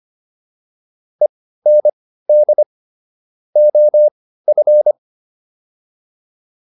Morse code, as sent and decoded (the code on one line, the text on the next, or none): . -. -.. / --- ..-.
END OF